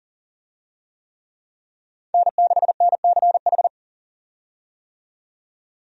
Morse code, heard as a keyed tone, about 40 wpm, 700 Hz. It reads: N6NCH
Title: USA Call Signs - 5 Characters - 40wpm